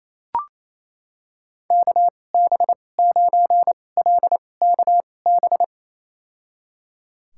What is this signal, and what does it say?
Morse code, heard as a keyed tone, about 28 wpm, spaced as for 20 wpm, 700 Hz. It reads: KB9LKB
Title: USA Call Signs - 6 Characters - 2x Character Spacing - 28wpm